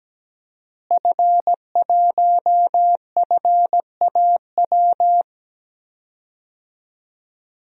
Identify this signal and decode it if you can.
Morse code, heard as a keyed tone, about 17 wpm, 700 Hz. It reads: F1FAW
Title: DXCC-100 Call Signs - 5 Characters - Repeated 3x - 17wpm